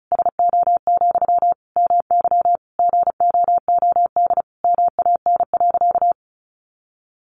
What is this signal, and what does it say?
Morse code, heard as a keyed tone, about 34 wpm, 700 Hz. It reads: SO, MY GOOD MAN.
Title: Sentences from Top 1200 Words 35wpm